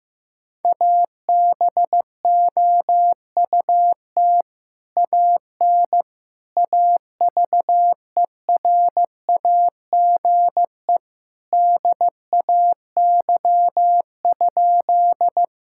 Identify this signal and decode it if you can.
Morse code, heard as a keyed tone, about 15 wpm, 700 Hz. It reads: ABOUT AN AVERAGE DAY?
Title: Sentences from Top 1000 Words - Review Part 6 - 15wpm